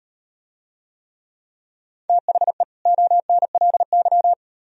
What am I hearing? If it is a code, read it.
Morse code, heard as a keyed tone, about 38 wpm, 700 Hz. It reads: THE ONLY